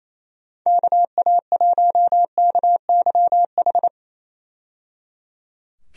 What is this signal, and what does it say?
Morse code, heard as a keyed tone, about 28 wpm, 700 Hz. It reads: KA1KYH
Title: USA Call Signs - 6 Characters - 28wpm